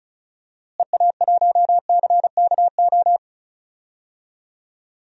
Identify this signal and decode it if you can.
Morse code, heard as a keyed tone, about 34 wpm, 700 Hz. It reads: EA1CKO
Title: DXCC-100 Call Signs - 6 Characters - Repeated 2x - 35wpm